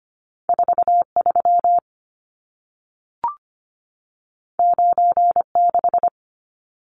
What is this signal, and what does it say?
Morse code, heard as a keyed tone, about 25 wpm, 700 Hz. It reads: 43 96
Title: Numbers - 2-Digit 25wpm